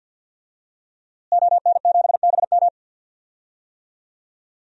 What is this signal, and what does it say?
Morse code, heard as a keyed tone, about 46 wpm, 700 Hz. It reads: ON7BM
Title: DXCC-100 Call Signs - 5 Characters - Repeated 3x - 50wpm